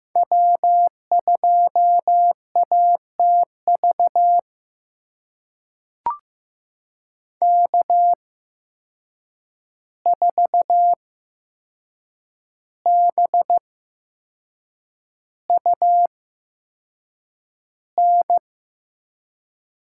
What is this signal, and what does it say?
Morse code, heard as a keyed tone, about 15 wpm, spaced as for 6 wpm, 700 Hz. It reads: W2ATV K4BUN
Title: USA Call Signs - 5 Characters - 8x Character Spacing - 15wpm